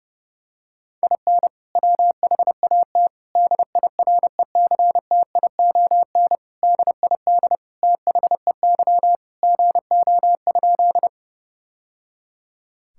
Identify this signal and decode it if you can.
Morse code, heard as a keyed tone, about 30 wpm, 700 Hz. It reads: IN WHAT DIRECTION DID THEY GO?
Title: Sentences from Top 1000 Words - Review Part 5 - 30wpm